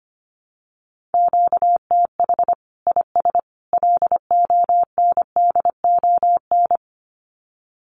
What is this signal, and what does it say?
Morse code, heard as a keyed tone, about 25 wpm, 700 Hz. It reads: QTH IS LONDON
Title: Common QSO Phrases 25wpm